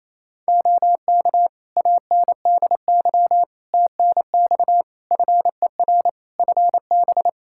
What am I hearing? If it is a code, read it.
Morse code, heard as a keyed tone, about 28 wpm, 700 Hz. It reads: OK ANDY TNX FER FB
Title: K1USN SST QSO Practice 28wpm